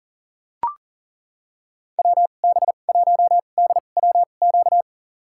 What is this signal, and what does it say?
Morse code, heard as a keyed tone, about 40 wpm, spaced as for 29 wpm, 700 Hz. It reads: WB1DWQ
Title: USA Call Signs - 6 Characters - 2x Character Spacing - 40wpm